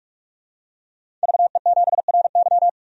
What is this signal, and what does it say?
Morse code, heard as a keyed tone, about 46 wpm, 700 Hz. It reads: VE7RY
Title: DXCC-100 Call Signs - 5 Characters - 45wpm